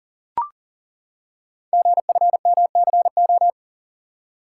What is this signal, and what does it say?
Morse code, heard as a keyed tone, about 42 wpm, 690 Hz. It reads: GFMCO
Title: Code Groups - 5 Letters - 40wpm